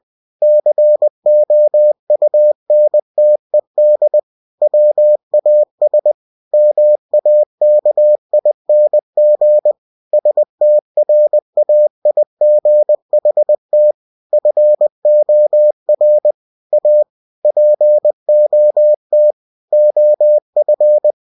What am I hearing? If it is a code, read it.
Morse code, heard as a keyed tone, about 20 wpm, 600 Hz. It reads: COUNTED WAS MAKING STRAIGHT FOR A POT OF
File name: cw-books / winnie-the-pooh / winnie-20wpm-page-0039